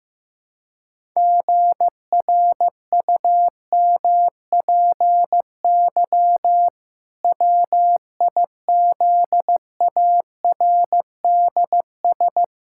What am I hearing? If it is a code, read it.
Morse code, heard as a keyed tone, about 16 wpm, 700 Hz. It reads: GRUMPY WIZARDS
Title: Pangram Challenge v2 - 15wpm